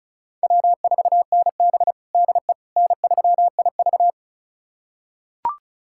Morse code, heard as a keyed tone, about 36 wpm, 700 Hz. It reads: W4NB DE N3IV
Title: USA Callsign DE Callsign - 4 Characters - 35wpm